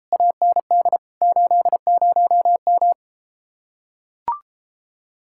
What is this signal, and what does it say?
Morse code, heard as a keyed tone, about 33 wpm, 700 Hz. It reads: AND 80M